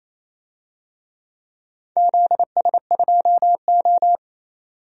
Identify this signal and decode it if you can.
Morse code, heard as a keyed tone, about 28 wpm, 700 Hz. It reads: ZS2O
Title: DXCC-100 Call Signs - 4 Characters - 28wpm